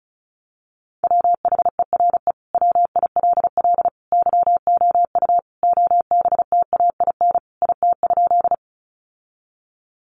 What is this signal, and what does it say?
Morse code, heard as a keyed tone, about 34 wpm, 700 Hz. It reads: WHERE WILL YOU OBTAIN IT?
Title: Sentences from Top 1300 Words 35wpm